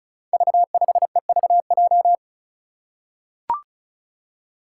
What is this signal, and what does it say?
Morse code, heard as a keyed tone, about 35 wpm, 700 Hz. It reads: V5EVJ
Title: Five Challenging Letter-Number Combinations 35wpm